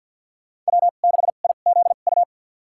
Morse code, heard as a keyed tone, about 52 wpm, spaced as for 36 wpm, 700 Hz. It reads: W6IZU
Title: USA Call Signs - 5 Characters - 2x Character Spacing - 50wpm